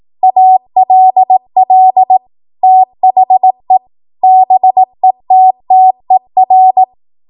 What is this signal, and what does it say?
Morse code, heard as a keyed tone, about 18 wpm, 750 Hz. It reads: ALL THE BETTER
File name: Morse / 18WPM